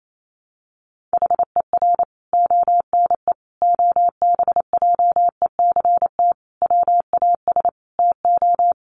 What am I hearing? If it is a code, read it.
Morse code, heard as a keyed tone, about 28 wpm, 700 Hz. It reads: HER ONE OBJECT WAS TO
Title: Sentences from Top 1500 Words 28wpm